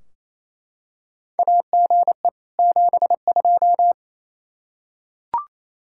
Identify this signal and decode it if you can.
Morse code, heard as a keyed tone, about 28 wpm, 700 Hz. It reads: AGE 72
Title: K1USN SST QSO Practice 28wpm